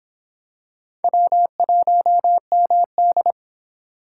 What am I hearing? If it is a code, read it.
Morse code, heard as a keyed tone, about 25 wpm, 700 Hz. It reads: W1MD